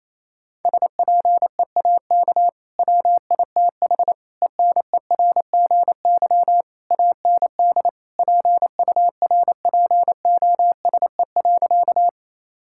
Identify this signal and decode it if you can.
Morse code, heard as a keyed tone, about 28 wpm, 700 Hz. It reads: SPEAK WITH ENERGY AND PURPOSE.
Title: Sentences from Top 1000 Words - Review Part 6 - 28wpm